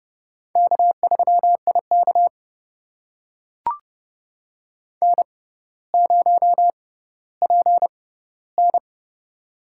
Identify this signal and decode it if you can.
Morse code, heard as a keyed tone, about 30 wpm, spaced as for 15 wpm, 700 Hz. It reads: K3IK N0PN
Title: USA Call Signs - 4 Characters - 6x Character Spacing - 30wpm